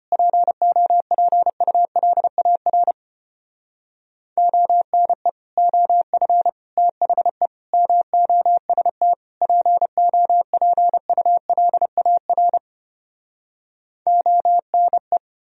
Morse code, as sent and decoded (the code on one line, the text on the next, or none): .--. --- .--. ..- .-.. .- .-. / --- -. / --- ..-. / - .... / -- --- ... - / .--. --- .--. ..- .-.. .- .-. / --- -. .
POPULAR ONE OF THE MOST POPULAR ONE